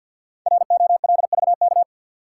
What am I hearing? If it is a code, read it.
Morse code, heard as a keyed tone, about 53 wpm, 700 Hz. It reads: ROLVK
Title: Code Groups - 5 Letter:Numbers - 50wpm